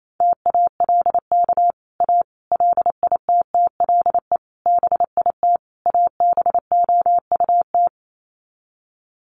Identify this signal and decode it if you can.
Morse code, heard as a keyed tone, about 28 wpm, 700 Hz. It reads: TALK A LITTLE BIT ABOUT
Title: Sets of 5 Words Encore 28wpm